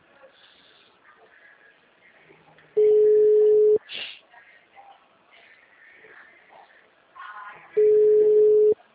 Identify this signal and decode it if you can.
Ringback tone (Europe, 425 Hz).
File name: out- -2005- - - .5237